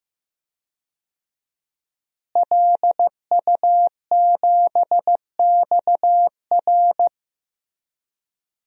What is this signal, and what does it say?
Morse code, heard as a keyed tone, about 15 wpm, 700 Hz. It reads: LU7XR
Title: DXCC-100 Call Signs - 5 Characters - Repeated 3x - 15wpm